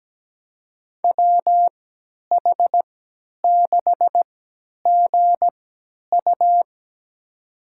Morse code, heard as a keyed tone, about 17 wpm, spaced as for 10 wpm, 700 Hz. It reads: WH6GU
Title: USA Call Signs - 5 Characters - 3x Character Spacing - 17wpm